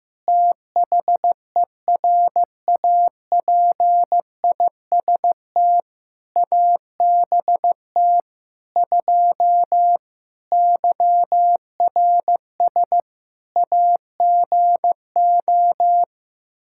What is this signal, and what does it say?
Morse code, heard as a keyed tone, about 15 wpm, 700 Hz. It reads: THERAPIST ABT 2 YRS AGO
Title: K1USN SST QSO Practice 15wpm